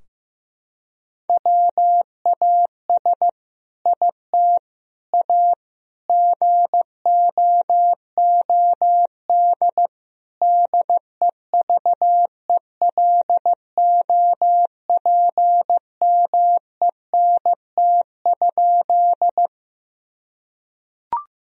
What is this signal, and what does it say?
Morse code, heard as a keyed tone, about 15 wpm, 700 Hz. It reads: WAS IT A GOOD DEVELOPMENT?
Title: Sentences from Top 1000 Words - Review Part 4 - 15wpm